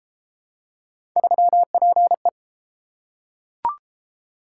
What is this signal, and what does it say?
Morse code, heard as a keyed tone, about 29 wpm, 700 Hz. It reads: 3PE